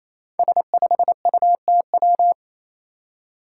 Morse code, heard as a keyed tone, about 28 wpm, 700 Hz. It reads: S5UTW
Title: Five Challenging Letter-Number Combinations 28wpm